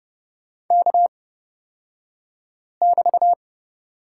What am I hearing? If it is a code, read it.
Morse code, heard as a keyed tone, about 30 wpm, spaced as for 5 wpm, 690 Hz. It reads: K<BT>
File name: Lesson 36d - BT - Two Characters 30wpm